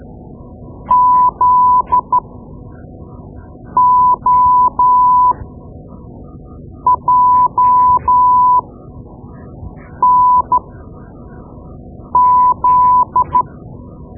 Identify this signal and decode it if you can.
Morse code, heard as a keyed tone, about 12 wpm, spaced as for 5 wpm, 1000 Hz. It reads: ZOJNZ
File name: Morse Lessons / Morse Lessons / Tape-2B